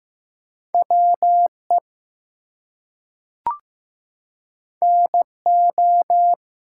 Morse code, heard as a keyed tone, about 15 wpm, 700 Hz. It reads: WE NO
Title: Two Letter Words - 15wpm